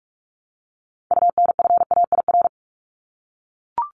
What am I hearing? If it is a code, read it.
Morse code, heard as a keyed tone, about 44 wpm, 700 Hz. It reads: UNFAIR
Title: Top 1751-2000 Words 45wpm